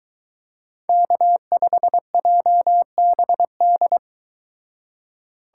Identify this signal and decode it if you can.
Morse code, heard as a keyed tone, about 22 wpm, 700 Hz. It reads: K5JBD